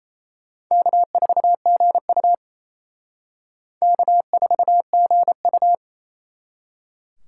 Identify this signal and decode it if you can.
Morse code, heard as a keyed tone, about 34 wpm, 700 Hz. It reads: K4GU K4GU
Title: Speed Racing - USA Call Signs 28wpm